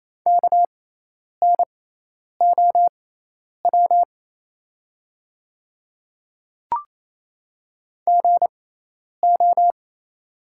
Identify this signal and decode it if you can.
Morse code, heard as a keyed tone, about 27 wpm, spaced as for 10 wpm, 700 Hz. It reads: KNOW GO
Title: 6x Character Spacing - Top 100 Words - 28wpm